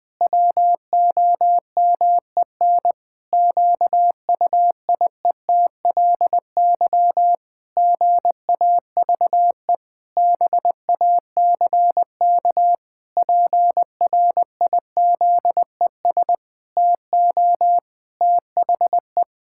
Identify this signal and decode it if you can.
Morse code, heard as a keyed tone, about 20 wpm, 700 Hz. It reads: WOMEN QUIETLY GAVE BACK PRIZES TO THE
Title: Pangram Challenge v2 - 20wpm